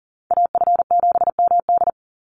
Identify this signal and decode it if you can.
Morse code, heard as a keyed tone, about 40 wpm, 700 Hz. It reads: AF7MD
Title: NA SOTA Chaser Callsigns 40wpm